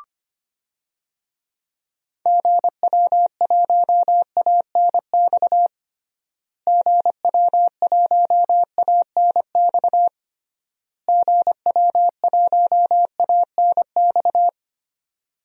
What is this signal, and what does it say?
Morse code, heard as a keyed tone, about 25 wpm, 700 Hz. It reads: GW1ANX GW1ANX GW1ANX
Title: DXCC-100 Call Signs - 6 Characters - Repeated 3x - 25wpm